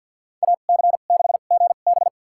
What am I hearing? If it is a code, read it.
Morse code, heard as a keyed tone, about 46 wpm, spaced as for 35 wpm, 700 Hz. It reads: AC6GB
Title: USA Call Signs - 5 Characters - 2x Character Spacing - 50wpm